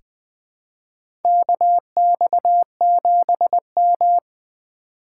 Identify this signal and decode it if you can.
Morse code, heard as a keyed tone, about 20 wpm, 700 Hz. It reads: KX7M
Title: K1USN SST QSO Practice - 20wpm